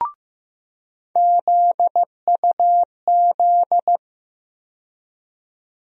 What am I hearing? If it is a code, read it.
Morse code, heard as a keyed tone, about 15 wpm, 700 Hz. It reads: ZUZ
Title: USA Call Sign Postfixes - 3 Letters 15wpm